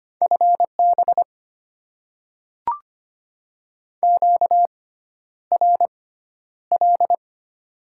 Morse code, heard as a keyed tone, about 25 wpm, spaced as for 11 wpm, 700 Hz. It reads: FB QRL
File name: Top 20 QSO Elements - 6x Character Spacing - 25wpm